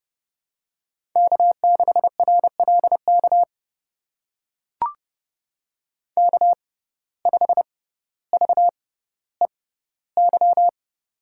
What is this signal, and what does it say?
Morse code, heard as a keyed tone, about 30 wpm, spaced as for 15 wpm, 700 Hz. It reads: K6RLK K5VEY
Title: USA Call Signs - 5 Characters - 6x Character Spacing - 30wpm